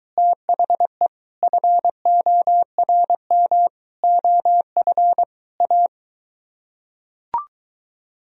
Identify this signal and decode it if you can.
Morse code, heard as a keyed tone, about 23 wpm, 700 Hz. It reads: THE FORM OF A